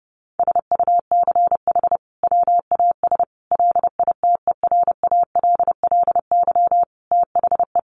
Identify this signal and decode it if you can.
Morse code, heard as a keyed tone, about 30 wpm, 700 Hz. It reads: SUCH WAS LITERALLY THE
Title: Sentences from Top 1400 Words 30wpm